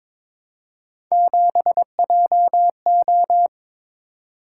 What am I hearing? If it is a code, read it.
Morse code, heard as a keyed tone, about 22 wpm, 700 Hz. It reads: 7JO
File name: practice / 22wpm / Three Letter-Number Combinations - 22wpm